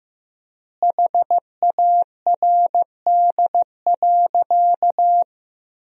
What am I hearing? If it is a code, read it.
Morse code, heard as a keyed tone, about 15 wpm, 700 Hz. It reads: HARD.
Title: 4x Word-Spacing - Sentences from Top 300 Words - 15wpm